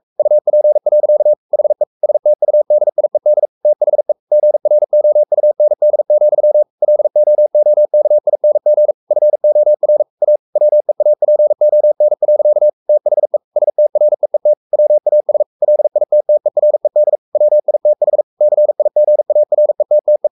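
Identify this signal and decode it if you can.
Morse code, heard as a keyed tone, about 44 wpm, 600 Hz. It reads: UP. HE STUDIED THE GROUND, LOOKING FOR A WEAPON. THE STREET WAS LITTERED WITH CIGARETTE
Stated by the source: Morse code at 43 wpm sending Ken Follett's novel Never in English, from CW med Gnister.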